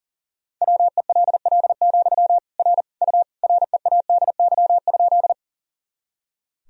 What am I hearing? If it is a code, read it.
Morse code, heard as a keyed tone, about 40 wpm, 700 Hz. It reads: WELL, R U READY?
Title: Sentences from Top 400 Words and Abbreviations 40wpm